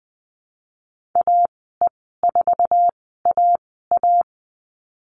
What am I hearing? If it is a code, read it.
Morse code, heard as a keyed tone, about 19 wpm, spaced as for 14 wpm, 700 Hz. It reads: AE4AA
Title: 069 Lesson 12g - Letter d - Call Signs 20wpm